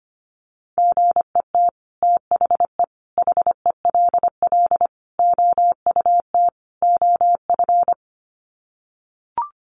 Morse code, heard as a keyed tone, about 25 wpm, 700 Hz. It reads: GET THE HELL OUT OF
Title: Sets of 5 Words Encore2 25wpm